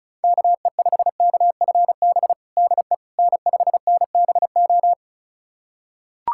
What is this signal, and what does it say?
Morse code, heard as a keyed tone, about 36 wpm, 700 Hz. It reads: KE5KFB DE N5NBO
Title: USA Callsign DE Callsign 35wpm